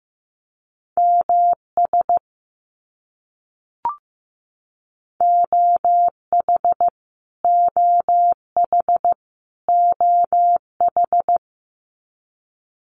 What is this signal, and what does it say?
Morse code, heard as a keyed tone, about 15 wpm, 700 Hz. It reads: MS OH OH OH
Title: US States Repeated 3x 15wpm